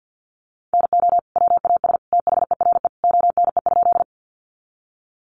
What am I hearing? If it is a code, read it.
Morse code, heard as a keyed tone, about 50 wpm, 700 Hz. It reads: NO WAS THERE ONE?
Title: Sentences from Top 100 Words 50wpm